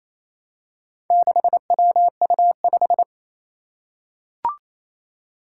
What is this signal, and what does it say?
Morse code, heard as a keyed tone, about 27 wpm, 700 Hz. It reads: 6WU5
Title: Four Challenging Letter-Number Combinations 28wpm